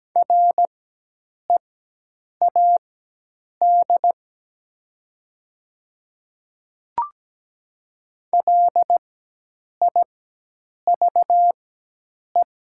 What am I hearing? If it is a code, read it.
Morse code, heard as a keyed tone, about 17 wpm, spaced as for 8 wpm, 700 Hz. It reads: READ LIVE
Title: 4x Character Spacing - Top 300 Words - 17wpm